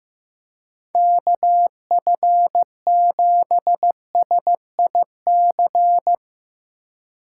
Morse code, heard as a keyed tone, about 15 wpm, 700 Hz. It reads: KF7SIC
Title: Speed Racing - USA Call Signs 15wpm